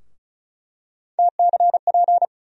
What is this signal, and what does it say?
Morse code, heard as a keyed tone, about 36 wpm, 700 Hz. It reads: TCP
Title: USA Call Sign Postfixes - 3 Letters 35wpm